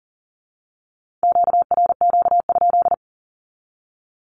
Morse code, heard as a keyed tone, about 38 wpm, 700 Hz. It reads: QRQ?
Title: Q-Codes with Spoken Meaning - 40wpm